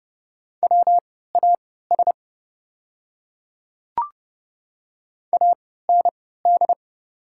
Morse code, heard as a keyed tone, about 30 wpm, spaced as for 17 wpm, 700 Hz. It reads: WAS AND
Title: 3x Character Spacing - Top 100 Words - 30wpm